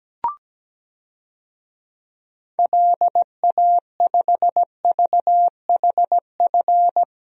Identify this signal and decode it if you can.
Morse code, heard as a keyed tone, about 17 wpm, 700 Hz. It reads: LA5VHF
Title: DXCC-100 Call Signs - 6 Characters - Repeated 3x - 17wpm